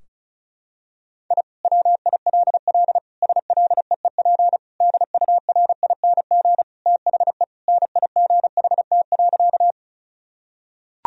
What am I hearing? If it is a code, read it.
Morse code, heard as a keyed tone, about 36 wpm, 700 Hz. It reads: I WILL SLEEP DURING THE NIGHT.
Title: Sentences from Top 1000 Words - Review Part 2 - 35wpm